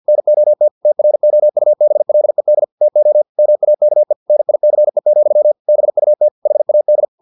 Morse code, heard as a keyed tone, about 50 wpm, 600 Hz. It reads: NOT TROUBLED TO MAKE NICE, BUT HAD
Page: Never 49wpm